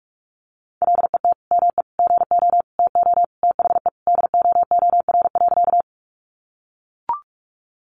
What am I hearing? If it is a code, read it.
Morse code, heard as a keyed tone, about 45 wpm, 700 Hz. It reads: LET ME GO TO THE DOOR.
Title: Sentences from Top 300 Words 45wpm